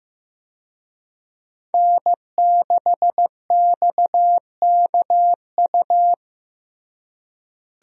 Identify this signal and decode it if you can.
Morse code, heard as a keyed tone, about 15 wpm, 700 Hz. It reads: N6XKU
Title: USA Call Signs - 5 Characters - Repeated 3x - 15wpm